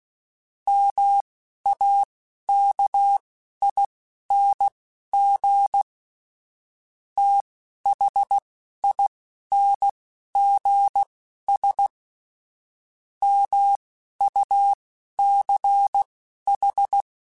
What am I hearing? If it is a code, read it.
Morse code, heard as a keyed tone, about 16 wpm, spaced as for 11 wpm, 800 Hz. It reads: MAKING THINGS MUCH
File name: Technical Subjects / Technical Subjects-mp3s / AND Gate @10